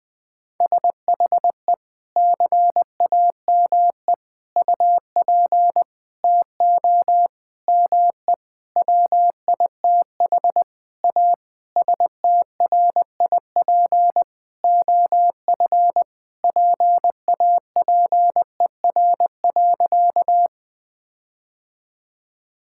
Morse code, heard as a keyed tone, about 20 wpm, 700 Hz. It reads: SHE CAME UP TO ME WITH A STRIP OF PAPER.
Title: Sentences from Top 1600 Words 20wpm